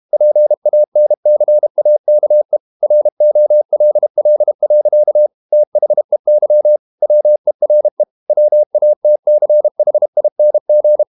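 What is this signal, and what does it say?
Morse code, heard as a keyed tone, about 32 wpm, 600 Hz. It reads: PANCAKE ROLL. THEY WERE WATCHING
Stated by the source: Morse code practice at 32 wpm from CW med Gnister, reading Ken Follett's novel Never in English.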